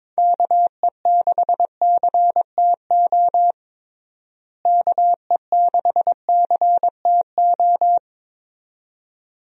Morse code, heard as a keyed tone, about 22 wpm, 700 Hz. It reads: KE6CTO KE6CTO
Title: USA Call Signs - 6 Characters - Repeated 2x - 22wpm